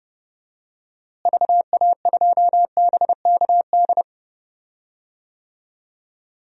Morse code, heard as a keyed tone, about 31 wpm, 700 Hz. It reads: VA2BKD
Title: DXCC-100 Call Signs - 6 Characters - Repeated 2x - 30wpm